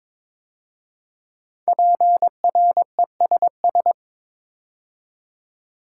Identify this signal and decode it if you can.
Morse code, heard as a keyed tone, about 22 wpm, 690 Hz. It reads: PRESS
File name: 5 Letter Words - First 3 Letters Given - 22wpm